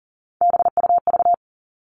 Morse code, heard as a keyed tone, about 40 wpm, 700 Hz. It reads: BUV
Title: All Cut Numbers - 3-Digit 40wpm